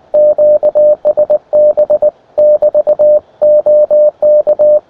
Morse code, heard as a keyed tone, about 21 wpm, 600 Hz. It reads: QSB <BT> OK
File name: attachments / ua0sbq 20m